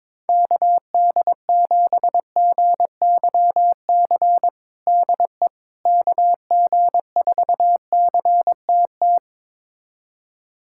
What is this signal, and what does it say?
Morse code, heard as a keyed tone, about 22 wpm, 700 Hz. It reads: KD7GYC DE KG4CTT
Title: USA Callsign DE Callsign - 6 Characters - 22wpm